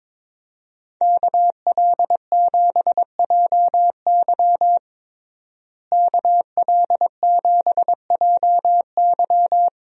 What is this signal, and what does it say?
Morse code, heard as a keyed tone, about 22 wpm, 700 Hz. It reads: KL7JY KL7JY
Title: DXCC-100 Call Signs - 5 Characters - Repeated 2x - 22wpm